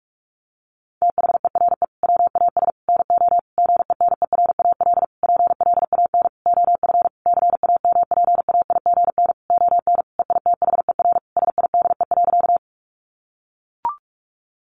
Morse code, heard as a keyed tone, about 44 wpm, 700 Hz. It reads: THERE WAS NO GENERAL PLAN OF CAMPAIGN ON EITHER SIDE.
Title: Sentences from Top 1200 Words 45wpm